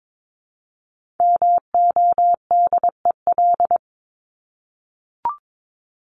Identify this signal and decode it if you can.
Morse code, heard as a keyed tone, about 22 wpm, 700 Hz. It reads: MODEL
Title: Top 501-750 Words 22wpm